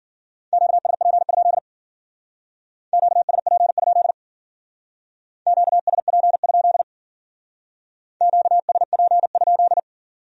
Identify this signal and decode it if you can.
Morse code, heard as a keyed tone, about 49 wpm, 700 Hz. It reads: QSP? QSP? QSP? QSP?
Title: Speed-Racing Q-Codes - 40wpm